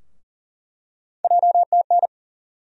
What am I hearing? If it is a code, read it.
Morse code, heard as a keyed tone, about 40 wpm, 700 Hz. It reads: JTN
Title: USA Call Sign Postfixes - 3 Letters 40wpm